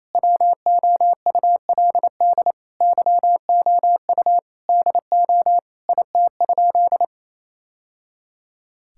Morse code, heard as a keyed tone, about 28 wpm, 700 Hz. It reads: WOULD YOU DO IT?
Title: Sentences from Top 1000 Words - Review Part 5 - 28wpm